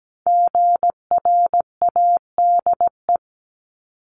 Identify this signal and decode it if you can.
Morse code, heard as a keyed tone, about 17 wpm, 700 Hz. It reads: GRADE
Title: Multi-letter Phonemes Repeated 3x 17wpm